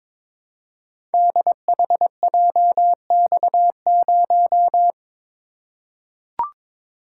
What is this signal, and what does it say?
Morse code, heard as a keyed tone, about 22 wpm, 700 Hz. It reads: DHJX0
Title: Code Groups - 5 Letter:Numbers - 22wpm